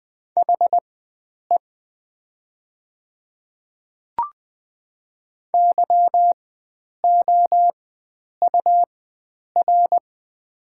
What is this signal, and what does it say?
Morse code, heard as a keyed tone, about 20 wpm, spaced as for 9 wpm, 700 Hz. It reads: HE YOUR
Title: 4x Character Spacing - Top 100 Words - 20wpm